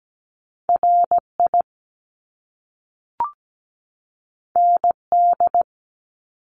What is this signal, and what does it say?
Morse code, heard as a keyed tone, about 17 wpm, 700 Hz. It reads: RI ND
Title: Top Two Letter Combinations in Words 17wpm